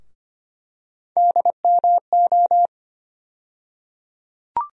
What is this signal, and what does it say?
Morse code, heard as a keyed tone, about 25 wpm, 700 Hz. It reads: DMO
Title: USA Call Sign Postfixes - 3 Letters 25wpm